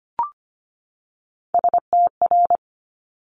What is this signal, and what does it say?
Morse code, heard as a keyed tone, about 25 wpm, 700 Hz. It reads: STR
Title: Multi-letter Phonemes 25wpm